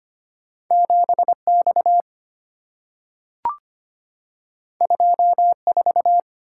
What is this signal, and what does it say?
Morse code, heard as a keyed tone, about 25 wpm, 700 Hz. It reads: 7X 24
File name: Two Letter-Number Combinations - 25wpm